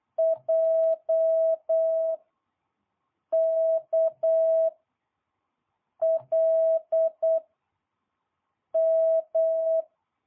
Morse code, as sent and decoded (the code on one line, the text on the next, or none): .--- / -.- / .-.. / --
J K L M